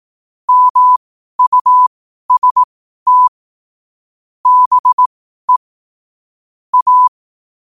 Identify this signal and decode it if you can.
Morse code, heard as a keyed tone, about 18 wpm, spaced as for 12 wpm, 1000 Hz. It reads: MUST BE A
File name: qotdfiles / 12wpm